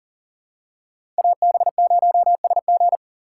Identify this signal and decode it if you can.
Morse code, heard as a keyed tone, about 40 wpm, 700 Hz. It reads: AB0SG
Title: USA Call Signs - 5 Characters - 4x Character Spacing - 40wpm